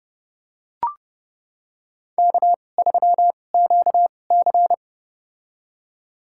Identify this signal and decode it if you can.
Morse code, heard as a keyed tone, about 29 wpm, spaced as for 21 wpm, 700 Hz. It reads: K3QC
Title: USA Call Signs - 2x Character Spacing - 30wpm